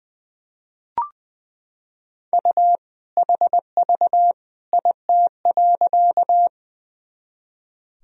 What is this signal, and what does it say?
Morse code, heard as a keyed tone, about 20 wpm, 700 Hz. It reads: U HV IT.
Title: Sentences from Top 100 Words and Abbreviations 20wpm